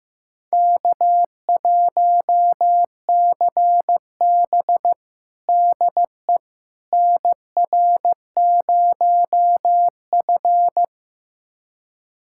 Morse code, heard as a keyed tone, about 15 wpm, 700 Hz. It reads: K1CB DE NR0F
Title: USA Callsign DE Callsign - 4 Characters - 15wpm